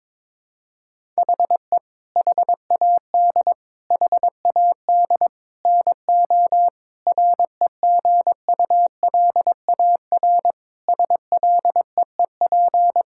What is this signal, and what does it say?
Morse code, heard as a keyed tone, about 22 wpm, 700 Hz. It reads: HE HAD HAD NO REGULAR SLEEP
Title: Sentences from Top 1000 Words - Review Part 6 - 22wpm